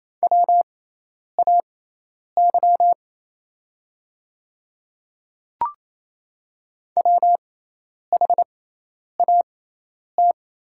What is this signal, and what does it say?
Morse code, heard as a keyed tone, about 28 wpm, spaced as for 10 wpm, 700 Hz. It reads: WAY WHAT
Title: 6x Character Spacing - Top 100 Words - 28wpm